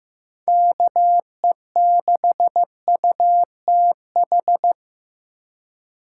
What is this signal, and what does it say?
Morse code, heard as a keyed tone, about 15 wpm, 700 Hz. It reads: KE6UTH
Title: USA Call Signs - 6 Characters - 3x Character Spacing - 15wpm